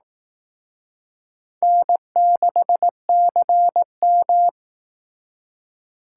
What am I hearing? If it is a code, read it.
Morse code, heard as a keyed tone, about 18 wpm, 700 Hz. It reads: N6CM
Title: Speed Racing - USA 1x2 Call Signs 15wpm